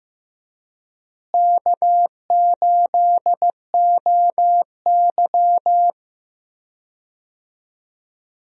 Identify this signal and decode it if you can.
Morse code, heard as a keyed tone, about 15 wpm, 700 Hz. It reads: K8OY
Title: USA Call Signs - 4 Characters - Repeated 3x - 15wpm